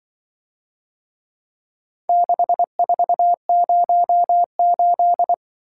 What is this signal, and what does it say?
Morse code, heard as a keyed tone, about 23 wpm, 700 Hz. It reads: 6408
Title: Speed-Racing - Four Letter-Number Combinations - 20wpm